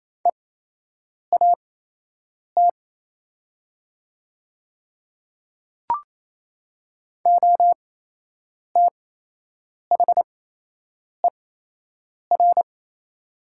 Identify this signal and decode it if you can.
Morse code, heard as a keyed tone, about 28 wpm, spaced as for 8 wpm, 700 Hz. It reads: EAT OTHER
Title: 8x Character Spacing - Top 300 Words - 28wpm